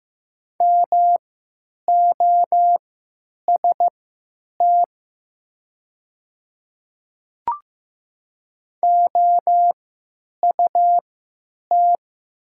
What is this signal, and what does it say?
Morse code, heard as a keyed tone, about 15 wpm, spaced as for 9 wpm, 700 Hz. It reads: MOST OUT
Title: 3x Character Spacing - Top 200 Words - 15wpm